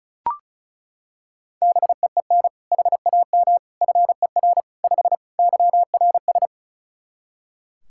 Morse code, heard as a keyed tone, about 34 wpm, 700 Hz. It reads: BEEN HAM FER 5 YRS
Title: K1USN SST QSO Practice 35wpm